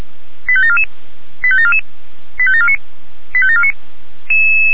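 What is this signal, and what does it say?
Five-tone selective call (ZVEI), heard as quick runs of five tones, tones of about 70 ms. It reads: 85744 85744 85749 85749
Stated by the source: ZVEI five-tone selective call received with an SDR.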